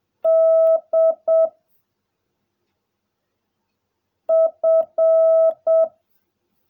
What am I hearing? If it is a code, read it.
Morse code, heard as a keyed tone, about 7 wpm, 660 Hz. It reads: D F
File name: jh8cbh / 2021cw / cw16a